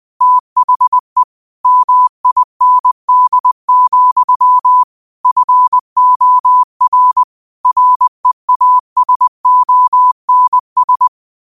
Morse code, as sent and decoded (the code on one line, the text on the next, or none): - .... / -- .. -. -.. --..-- / ..-. --- .-. / .-. .- ... --- -. ...
THE MIND, FOR REASONS